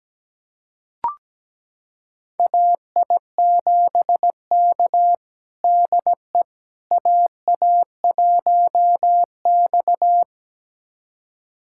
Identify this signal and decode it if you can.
Morse code, heard as a keyed tone, about 17 wpm, 700 Hz. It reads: AI7K DE AA1X
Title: USA Callsign DE Callsign - 4 Characters - 17wpm